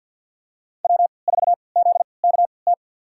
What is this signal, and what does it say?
Morse code, heard as a keyed tone, about 53 wpm, spaced as for 29 wpm, 700 Hz. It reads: W4ZKT
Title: USA Call Signs - 5 Characters - 3x Character Spacing - 50wpm